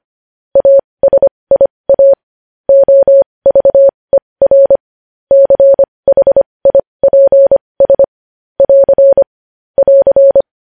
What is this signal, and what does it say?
Morse code, heard as a keyed tone, about 25 wpm, spaced as for 20 wpm, 550 Hz. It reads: ASIA OVER CHIPS <AR> <AR>
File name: podcast / News headlines-1x-25x20